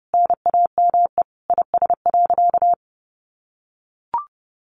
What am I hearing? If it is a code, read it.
Morse code, heard as a keyed tone, about 30 wpm, 700 Hz. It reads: NAME IS.
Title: Sentences from Top 1100 Words 30wpm